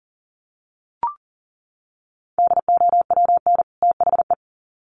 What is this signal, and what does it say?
Morse code, heard as a keyed tone, about 40 wpm, 700 Hz. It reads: DOWN THE
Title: Sets of 2 Words 40wpm